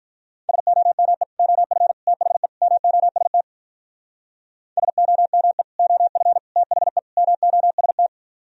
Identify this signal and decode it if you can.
Morse code, heard as a keyed tone, about 44 wpm, 700 Hz. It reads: SOME OF THE MOST SOME OF THE MOST